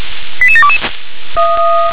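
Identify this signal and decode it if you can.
Five-tone selective call (ZVEI), heard as quick runs of five tones, tones of about 70 ms. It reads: 88722
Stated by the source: ZVEI five-tone call received on SDR.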